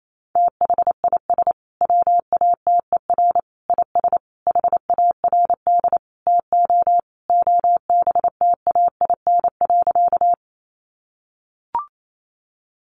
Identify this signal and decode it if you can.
Morse code, heard as a keyed tone, about 28 wpm, 700 Hz. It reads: THIS WATER IS HARD TO OBTAIN.
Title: Sentences from Top 1300 Words 28wpm